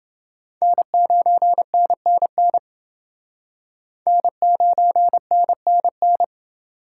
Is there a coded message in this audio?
Yes: Morse code from 0.6 to 6.2 s, heard as a keyed tone, about 28 wpm, 700 Hz. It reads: N9NNN N9NNN